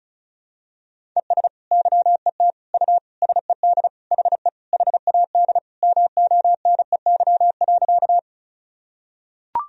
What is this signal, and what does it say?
Morse code, heard as a keyed tone, about 36 wpm, 700 Hz. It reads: ES YET U SED HE HAD MONEY.
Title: Sentences from Top 500 Words and Abbreviations 35wpm